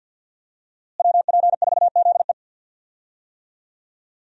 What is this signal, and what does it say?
Morse code, heard as a keyed tone, about 52 wpm, 700 Hz. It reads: WP4ZE
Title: USA Call Signs - 5 Characters - Repeated 3x - 50wpm